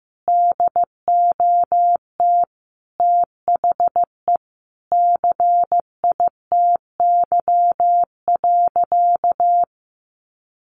Morse code, heard as a keyed tone, about 14 wpm, 700 Hz. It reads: DOT THE CITY.
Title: Sentences from Top 1500 Words 15wpm